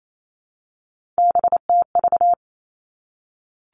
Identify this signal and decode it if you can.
Morse code, heard as a keyed tone, about 28 wpm, 700 Hz. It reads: BTV
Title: Three Challenging Letter-Number Combinations 28wpm